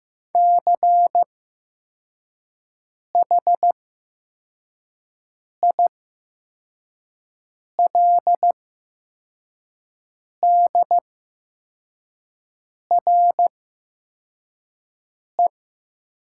Morse code, heard as a keyed tone, about 15 wpm, spaced as for 4 wpm, 700 Hz. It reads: CHILDRE
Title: 8x Character Spacing - Top 500 Words - 15wpm